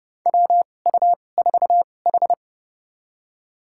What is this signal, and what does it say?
Morse code, heard as a keyed tone, about 32 wpm, spaced as for 22 wpm, 700 Hz. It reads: WU4H
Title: USA Call Signs - 2x Character Spacing - 30wpm